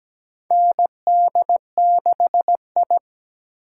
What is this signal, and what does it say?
Morse code, heard as a keyed tone, about 17 wpm, 700 Hz. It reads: ND6I
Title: USA Call Signs - 4 Characters - 3x Character Spacing - 17wpm